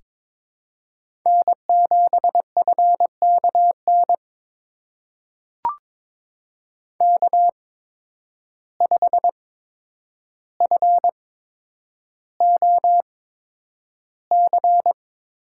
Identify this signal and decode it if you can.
Morse code, heard as a keyed tone, about 22 wpm, spaced as for 9 wpm, 700 Hz. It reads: N7FKN K5FOC
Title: USA Call Signs - 5 Characters - 8x Character Spacing - 22wpm